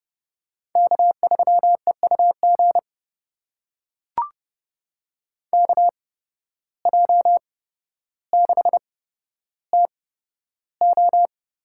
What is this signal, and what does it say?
Morse code, heard as a keyed tone, about 30 wpm, spaced as for 13 wpm, 700 Hz. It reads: K3EUG KJ6TO
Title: USA Call Signs - 5 Characters - 8x Character Spacing - 30wpm